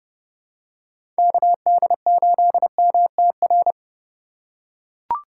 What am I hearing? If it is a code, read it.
Morse code, heard as a keyed tone, about 30 wpm, 700 Hz. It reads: KD8MTR
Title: USA Call Signs - 3x Character Spacing - 30wpm